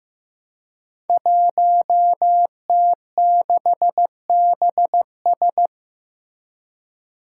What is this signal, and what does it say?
Morse code, heard as a keyed tone, about 15 wpm, 700 Hz. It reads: 1T6BS
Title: Five Challenging Letter-Number Combinations 15wpm